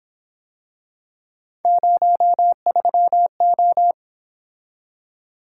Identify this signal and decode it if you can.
Morse code, heard as a keyed tone, about 26 wpm, 700 Hz. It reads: 03O
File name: Speed-Racing - Three Letter-Number Combinations - 22wpm